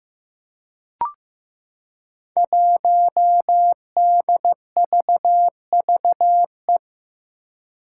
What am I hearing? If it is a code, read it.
Morse code, heard as a keyed tone, about 15 wpm, 700 Hz. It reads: 1DVVE
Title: Five Challenging Letter-Number Combinations 15wpm